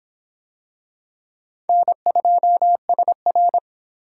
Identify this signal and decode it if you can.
Morse code, heard as a keyed tone, about 24 wpm, 700 Hz. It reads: N2SR